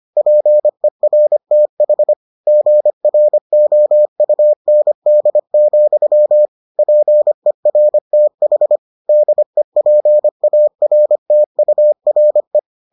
Morse code, heard as a keyed tone, about 25 wpm, 600 Hz. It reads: PERTH GROUND, PERTH DEPARTURE